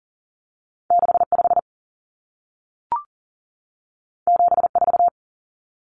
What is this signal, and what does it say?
Morse code, heard as a keyed tone, about 40 wpm, 700 Hz. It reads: 65 74
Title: Numbers - 2-Digit 40wpm